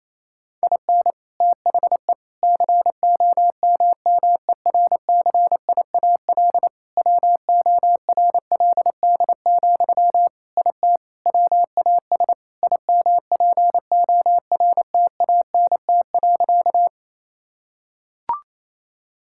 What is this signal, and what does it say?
Morse code, heard as a keyed tone, about 27 wpm, 700 Hz. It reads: IN THE COMMERCIAL WORLD, IT WAS IMPORTANT.
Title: Sentences from Top 1000 Words - Review Part 5 - 28wpm